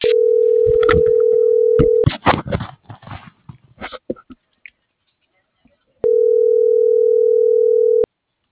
Ringback tone (North America), 440 + 480 Hz.